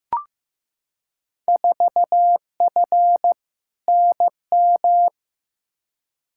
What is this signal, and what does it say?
Morse code, heard as a keyed tone, about 15 wpm, 700 Hz. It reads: 4F NM
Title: Field Day Exchanges Encore - 15wpm